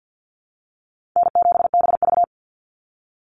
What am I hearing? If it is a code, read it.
Morse code, heard as a keyed tone, about 50 wpm, 700 Hz. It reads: N7BV
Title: Call Signs Repeated 2x - 50wpm